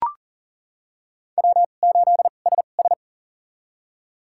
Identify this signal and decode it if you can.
Morse code, heard as a keyed tone, about 37 wpm, spaced as for 28 wpm, 700 Hz. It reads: W8SS
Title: USA Call Signs - 4 Characters - 2x Character Spacing - 40wpm